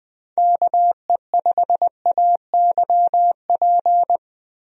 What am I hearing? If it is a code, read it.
Morse code, heard as a keyed tone, about 21 wpm, 700 Hz. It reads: KE5AYP
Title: USA Call Signs - 6 Characters - 20wpm